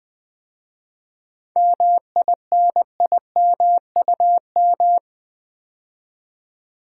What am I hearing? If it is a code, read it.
Morse code, heard as a keyed tone, about 20 wpm, 700 Hz. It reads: MINIMUM
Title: 7 Letter Words - First 3 Letters Given - 20wpm